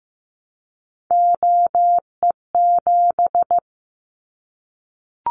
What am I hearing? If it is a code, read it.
Morse code, heard as a keyed tone, about 15 wpm, 700 Hz. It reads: OE7
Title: Callsign Prefixes for Top 100 DXCC - 15wpm